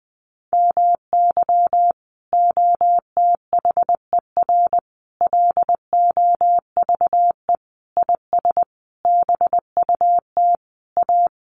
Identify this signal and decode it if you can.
Morse code, heard as a keyed tone, about 20 wpm, 700 Hz. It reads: MY OTHER LOVE IS BUT A
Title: Sentences from Top 1300 Words 20wpm